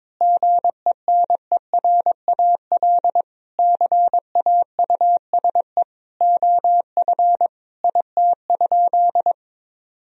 Morse code, heard as a keyed tone, about 22 wpm, 700 Hz. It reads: GENERAL CAUSE OF IT?
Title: Sentences from Top 1000 Words - Review Part 3 - 22wpm